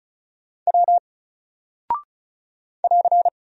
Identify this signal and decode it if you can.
Morse code, heard as a keyed tone, about 36 wpm, 700 Hz. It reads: W <AR>
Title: Single Character (Letter:Number:Punctuation:Prosign) 35wpm